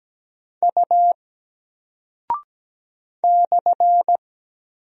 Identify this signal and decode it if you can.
Morse code, heard as a keyed tone, about 17 wpm, 700 Hz. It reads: U /
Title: Single Character (Letter:Number:Punctuation:Prosign) Alt 17wpm